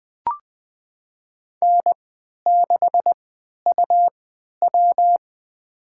Morse code, heard as a keyed tone, about 20 wpm, spaced as for 11 wpm, 700 Hz. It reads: N6UW
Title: USA Call Signs - 3x Character Spacing - 20wpm